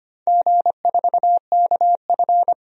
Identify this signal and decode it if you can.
Morse code, heard as a keyed tone, about 25 wpm, 700 Hz. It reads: G4KF
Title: DXCC-100 Call Signs - 4 Characters - 25wpm